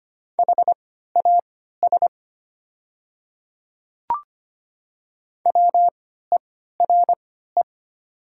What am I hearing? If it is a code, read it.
Morse code, heard as a keyed tone, about 25 wpm, spaced as for 14 wpm, 690 Hz. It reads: HAS WERE
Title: 3x Character Spacing - Top 300 Words - 25wpm